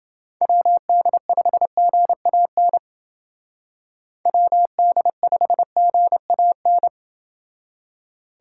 Morse code, heard as a keyed tone, about 29 wpm, 700 Hz. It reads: WD5GAN WD5GAN